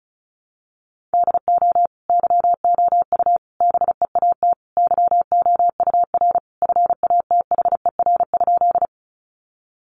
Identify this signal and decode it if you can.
Morse code, heard as a keyed tone, about 34 wpm, 700 Hz. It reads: DO YOU BEAT YOUR FATHER?
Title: Sentences from Top 1300 Words 35wpm